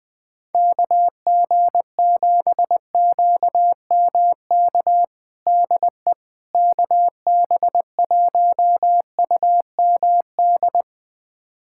Morse code, heard as a keyed tone, about 19 wpm, 700 Hz. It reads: KG7QMK DE KB1UMD
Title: USA Callsign DE Callsign - 6 Characters - 20wpm